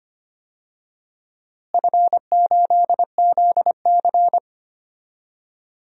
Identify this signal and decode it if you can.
Morse code, heard as a keyed tone, about 25 wpm, 700 Hz. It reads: F8ZC
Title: DXCC-100 Call Signs - 4 Characters - Repeated 3x - 25wpm